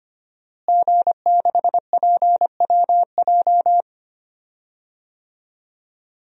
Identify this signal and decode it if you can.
Morse code, heard as a keyed tone, about 25 wpm, 700 Hz. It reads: G6PWJ
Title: DXCC-100 Call Signs - 5 Characters - Repeated 3x - 25wpm